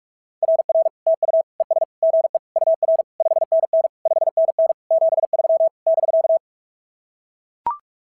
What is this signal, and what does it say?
Morse code, heard as a keyed tone, about 44 wpm, 690 Hz. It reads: RR TU ES GE UR 5NN 5NN 73 <BK>
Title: SOTA QSO Practice 45wpm